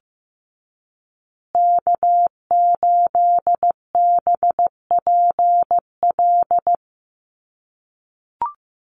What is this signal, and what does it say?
Morse code, heard as a keyed tone, about 15 wpm, 700 Hz. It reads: K8BPL
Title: USA Call Signs Repeated 2x 15wpm